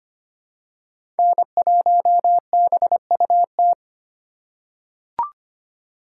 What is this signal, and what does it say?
Morse code, heard as a keyed tone, about 25 wpm, 700 Hz. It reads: N1BUT
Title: USA Call Signs - 5 Characters - 4x Character Spacing - 25wpm